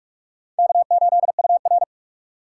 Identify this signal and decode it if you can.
Morse code, heard as a keyed tone, about 44 wpm, 690 Hz. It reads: K8UR
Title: USA Call Signs - 4 Characters - 3x Character Spacing - 45wpm